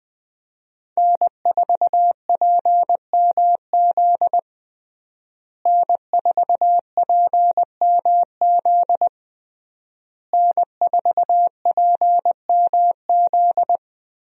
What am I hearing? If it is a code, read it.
Morse code, heard as a keyed tone, about 20 wpm, 700 Hz. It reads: N4PMZ N4PMZ N4PMZ
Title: USA Call Signs - 5 Characters - Repeated 3x - 20wpm